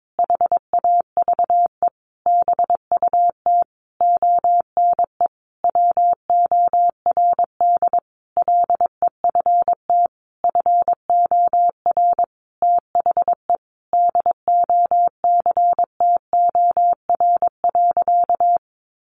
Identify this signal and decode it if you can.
Morse code, heard as a keyed tone, about 22 wpm, 700 Hz. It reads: HAVE BUT ONE WORD LEFT FOR THE DOCTOR.
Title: Sentences from Top 1500 Words 22wpm